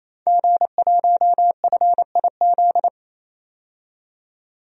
Morse code, heard as a keyed tone, about 28 wpm, 700 Hz. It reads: G1FIZ
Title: DXCC-100 Call Signs - 5 Characters - 28wpm